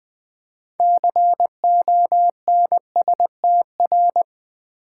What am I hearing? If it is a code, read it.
Morse code, heard as a keyed tone, about 20 wpm, 700 Hz. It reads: CONSTR
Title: Build A Word - 6 Letters At A Time - 12-Letter Words - 20wpm